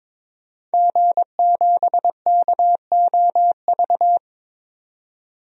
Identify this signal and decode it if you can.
Morse code, heard as a keyed tone, about 22 wpm, 700 Hz. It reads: G7KOV
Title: DXCC-100 Call Signs - 5 Characters - 22wpm